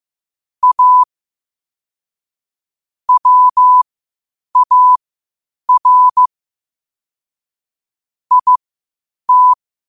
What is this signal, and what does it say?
Morse code, heard as a keyed tone, about 15 wpm, spaced as for 8 wpm, 1000 Hz. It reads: A WAR IT